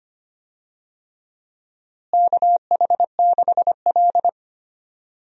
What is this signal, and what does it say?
Morse code, heard as a keyed tone, about 25 wpm, 700 Hz. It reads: KH6L
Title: DXCC-100 Call Signs - 4 Characters - Repeated 3x - 25wpm